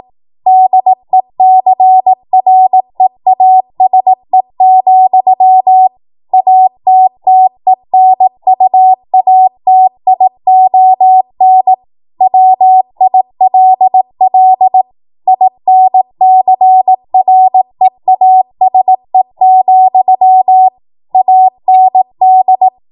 Morse code, heard as a keyed tone, about 18 wpm, 750 Hz. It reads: DECREASE, ATTENUATION WILL INCREASE, AND
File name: Morse / 18WPM